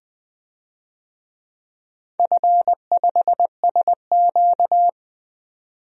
Morse code, heard as a keyed tone, about 21 wpm, 700 Hz. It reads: F5SQ
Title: DXCC-100 Call Signs - 4 Characters - 20wpm